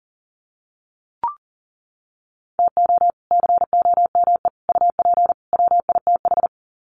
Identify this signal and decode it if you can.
Morse code, heard as a keyed tone, about 40 wpm, 700 Hz. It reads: TO COME UP WITH
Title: Sets of 4 Words 40wpm